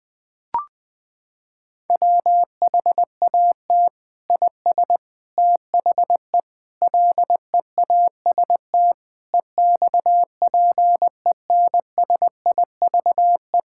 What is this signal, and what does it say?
Morse code, heard as a keyed tone, about 19 wpm, 700 Hz. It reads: WHAT IS THE LEAST EXPENSIVE
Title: Sentences from Top 1000 Words - Review Part 6 - 20wpm